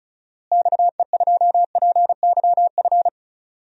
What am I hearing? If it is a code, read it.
Morse code, heard as a keyed tone, about 36 wpm, 700 Hz. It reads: XE2PYF
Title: DXCC-100 Call Signs - 6 Characters - Repeated 2x - 35wpm